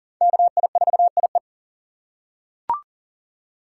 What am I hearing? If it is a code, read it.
Morse code, heard as a keyed tone, about 37 wpm, 700 Hz. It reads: KI4IE
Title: USA Call Signs - 5 Characters - 3x Character Spacing - 40wpm